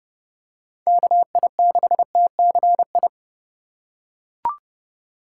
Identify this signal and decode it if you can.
Morse code, heard as a keyed tone, about 32 wpm, 700 Hz. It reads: KI6TCI
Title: USA Call Signs - 2x Character Spacing - 30wpm